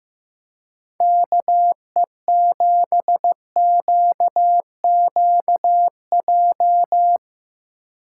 Morse code, heard as a keyed tone, about 16 wpm, 700 Hz. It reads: KE7QQJ
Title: Speed Racing - USA 2x3 Call Signs 15wpm